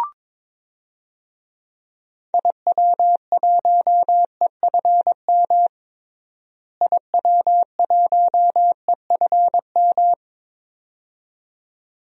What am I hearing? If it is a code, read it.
Morse code, heard as a keyed tone, about 21 wpm, 700 Hz. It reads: IW1EFM IW1EFM
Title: DXCC-100 Call Signs - 6 Characters - Repeated 2x - 22wpm